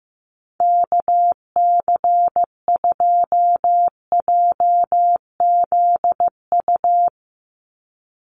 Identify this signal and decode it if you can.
Morse code, heard as a keyed tone, about 15 wpm, 700 Hz. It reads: KC2JZU
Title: USA Call Signs 15wpm